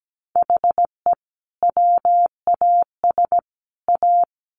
Morse code, heard as a keyed tone, about 17 wpm, 700 Hz. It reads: HE WAS A
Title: Sets of 3 Words 17wpm